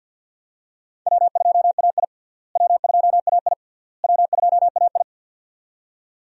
Morse code, heard as a keyed tone, about 46 wpm, 700 Hz. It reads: W2AI W2AI W2AI
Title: USA Call Signs - 4 Characters - Repeated 3x - 50wpm